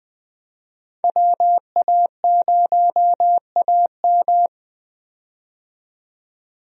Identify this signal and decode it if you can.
Morse code, heard as a keyed tone, about 20 wpm, 700 Hz. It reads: WA0AM
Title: USA Call Signs - 5 Characters - 20wpm